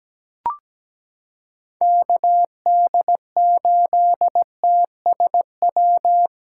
Morse code, heard as a keyed tone, about 17 wpm, 700 Hz. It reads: KD8TSW
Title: USA Callsign DE Callsign - 6 Characters - 17wpm